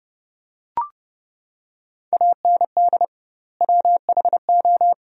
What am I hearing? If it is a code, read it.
Morse code, heard as a keyed tone, about 30 wpm, 700 Hz. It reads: AND WHO
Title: 2x Word-Spacing - Sentences from Top 100 Words - 30wpm